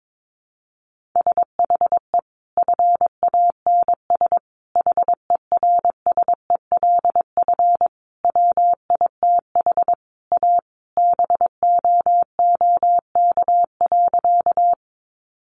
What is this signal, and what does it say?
Morse code, heard as a keyed tone, about 22 wpm, 700 Hz. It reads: SHE FANS HERSELF WITH A BOOK.
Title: Easy Phrases 22wpm